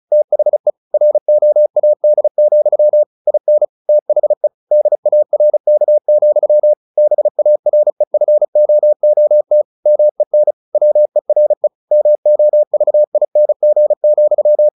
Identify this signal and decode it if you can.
Morse code, heard as a keyed tone, about 34 wpm, 600 Hz. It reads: THE ROAD, IN THE DARK, BAREFOOT MEN WERE MOVING,
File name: the old man and the sea / the old man and the sea 1 35wpm